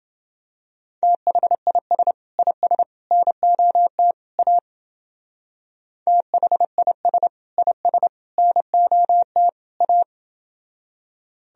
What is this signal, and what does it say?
Morse code, heard as a keyed tone, about 29 wpm, 700 Hz. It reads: THIS IS NOT A THIS IS NOT A